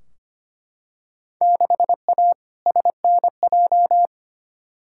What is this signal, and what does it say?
Morse code, heard as a keyed tone, about 25 wpm, 700 Hz. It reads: 6A SNJ
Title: Field Day Exchanges Encore - 25wpm